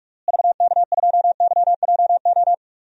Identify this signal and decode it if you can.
Morse code, heard as a keyed tone, about 45 wpm, 700 Hz. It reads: VK2YJO
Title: DXCC-100 Call Signs - 6 Characters - Repeated 2x - 45wpm